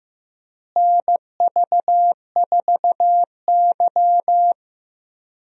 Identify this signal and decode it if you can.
Morse code, heard as a keyed tone, about 15 wpm, 700 Hz. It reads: NV4Y
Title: USA Call Signs - 4 Characters - Repeated 3x - 15wpm